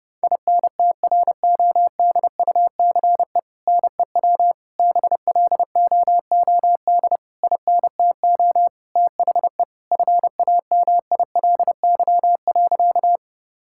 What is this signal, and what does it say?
Morse code, heard as a keyed tone, about 30 wpm, 700 Hz. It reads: INTRODUCE NEW BLOOD INTO THE FAMILY.
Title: Sentences from Top 1600 Words 30wpm